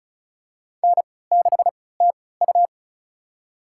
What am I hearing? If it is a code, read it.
Morse code, heard as a keyed tone, about 36 wpm, spaced as for 20 wpm, 700 Hz. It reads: N6TU
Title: USA Call Signs - 4 Characters - 3x Character Spacing - 35wpm